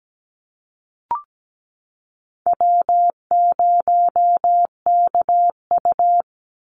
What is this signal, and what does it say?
Morse code, heard as a keyed tone, about 17 wpm, 700 Hz. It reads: W0KU
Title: Easy Call Signs - 17wpm